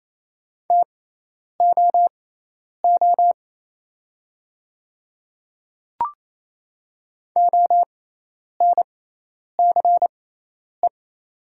Morse code, heard as a keyed tone, about 27 wpm, spaced as for 10 wpm, 690 Hz. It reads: TOO ONCE
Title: 6x Character Spacing - Top 300 Words - 28wpm